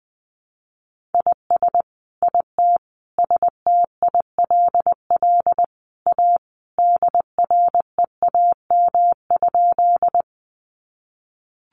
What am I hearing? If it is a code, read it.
Morse code, heard as a keyed tone, about 20 wpm, 700 Hz. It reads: IS IT STILL A DREAM?
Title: Sentences from Top 1500 Words 20wpm